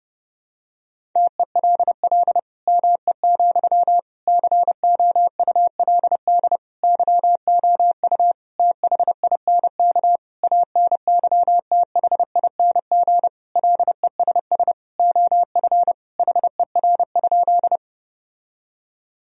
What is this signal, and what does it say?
Morse code, heard as a keyed tone, about 30 wpm, 700 Hz. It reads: TELL ME, COULD YOU THINK ANYTHING LESS OF HER?
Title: Sentences from Top 1000 Words - Review Part 2 - 30wpm